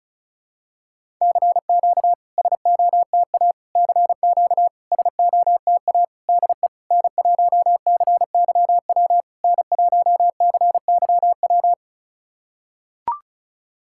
Morse code, heard as a keyed tone, about 36 wpm, 690 Hz. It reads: CQ SOTA CQ SOTA DE N1CYW N1CYW
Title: SOTA QSO Practice 35wpm